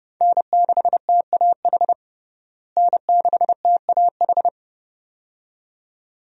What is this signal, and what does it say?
Morse code, heard as a keyed tone, about 30 wpm, 700 Hz. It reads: N6TAH N6TAH
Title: USA Call Signs - 5 Characters - Repeated 2x - 30wpm